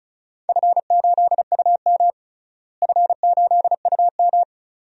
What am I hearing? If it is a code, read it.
Morse code, heard as a keyed tone, about 36 wpm, 700 Hz. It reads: F8UM F8UM
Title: DXCC-100 Call Signs - 4 Characters - Repeated 2x - 35wpm